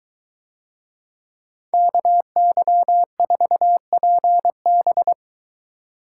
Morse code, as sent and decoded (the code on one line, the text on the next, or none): -.- -.-- ....- .--. -...
KY4PB